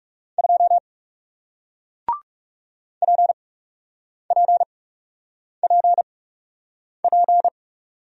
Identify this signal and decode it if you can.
Morse code, heard as a keyed tone, about 40 wpm, spaced as for 8 wpm, 700 Hz. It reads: 2 PPPP